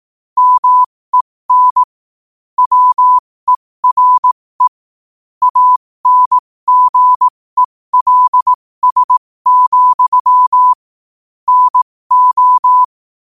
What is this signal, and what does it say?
Morse code, heard as a keyed tone, about 18 wpm, spaced as for 15 wpm, 1000 Hz. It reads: MEN WERE ANGELS, NO
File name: qotdfiles / 15wpm